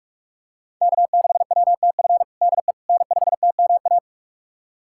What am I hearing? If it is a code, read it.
Morse code, heard as a keyed tone, about 44 wpm, 700 Hz. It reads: K6WTF DE N5TMA
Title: USA Callsign DE Callsign - 5 Characters - 45wpm